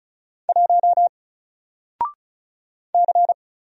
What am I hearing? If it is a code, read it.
Morse code, heard as a keyed tone, about 33 wpm, 700 Hz. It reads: 1 C
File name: Single-Letter-Number-35wpm